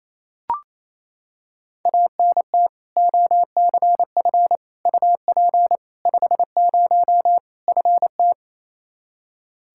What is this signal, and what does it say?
Morse code, heard as a keyed tone, about 28 wpm, 700 Hz. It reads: ANT OCF UP 50 FT
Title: Practice QSO - Exchange 2 - Info - 28wpm